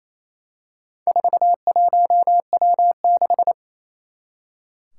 Morse code, heard as a keyed tone, about 28 wpm, 700 Hz. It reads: 41W6
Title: Four Challenging Letter-Number Combinations 28wpm